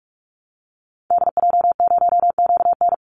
Morse code, heard as a keyed tone, about 44 wpm, 700 Hz. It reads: DJ0QN
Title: Call Signs Repeated 3x - 45wpm